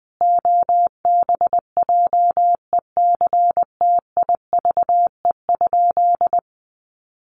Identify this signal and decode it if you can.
Morse code, heard as a keyed tone, about 20 wpm, 700 Hz. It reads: OBJECTIVE?
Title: Sentences from Top 1200 Words 20wpm